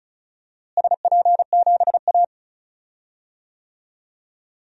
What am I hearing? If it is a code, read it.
Morse code, heard as a keyed tone, about 34 wpm, 700 Hz. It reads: SP7A
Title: DXCC-100 Call Signs - 4 Characters - Repeated 3x - 35wpm